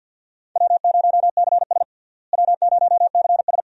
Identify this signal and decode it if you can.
Morse code, heard as a keyed tone, about 49 wpm, 700 Hz. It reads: W0CS W0CS
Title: USA Call Signs - 4 Characters - Repeated 2x - 50wpm